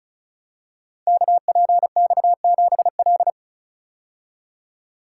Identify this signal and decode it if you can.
Morse code, heard as a keyed tone, about 36 wpm, 700 Hz. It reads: KPX7L
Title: Code Groups - 5 Letter:Numbers - 35wpm